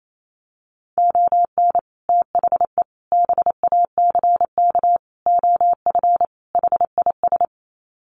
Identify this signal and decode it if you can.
Morse code, heard as a keyed tone, about 28 wpm, 700 Hz. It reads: ON THE BACK OF HIS
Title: Sets of 5 Words Encore 28wpm